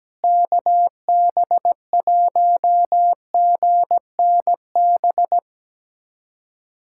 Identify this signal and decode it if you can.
Morse code, heard as a keyed tone, about 17 wpm, 700 Hz. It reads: KB1GNB
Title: USA Callsign DE Callsign 17wpm